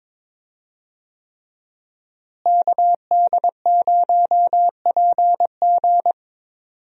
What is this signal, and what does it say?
Morse code, heard as a keyed tone, about 22 wpm, 700 Hz. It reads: KD0PG
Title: USA Call Signs - 5 Characters - Repeated 3x - 22wpm